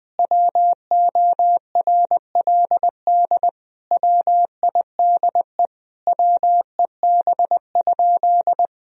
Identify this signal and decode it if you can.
Morse code, heard as a keyed tone, about 21 wpm, 700 Hz. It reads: WORLD WIDE WEB?
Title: Sentences from Top 1000 Words - Review Part 4 - 20wpm